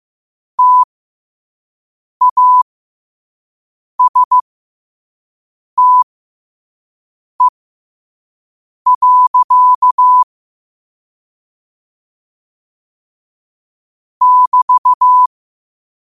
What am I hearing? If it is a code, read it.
Morse code, heard as a keyed tone, about 15 wpm, spaced as for 5 wpm, 1000 Hz. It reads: TASTE. <BT>
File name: qotdfiles / 5wpm